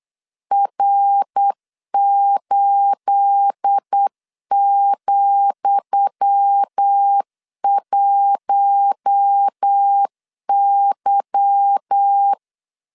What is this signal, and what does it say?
Morse code, heard as a keyed tone, about 8 wpm, 800 Hz. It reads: R8,1Y
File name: LEZIONI / 10 SIGNS / 10D COMA 8wpm 3749